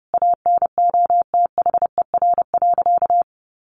Morse code, heard as a keyed tone, about 30 wpm, 700 Hz. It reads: ANOTHER.
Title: Sentences from Top 1400 Words 30wpm